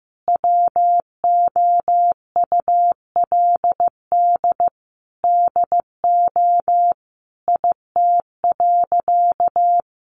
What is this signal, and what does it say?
Morse code, heard as a keyed tone, about 15 wpm, 700 Hz. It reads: WOULD DO IT.
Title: Sentences from Top 1500 Words 15wpm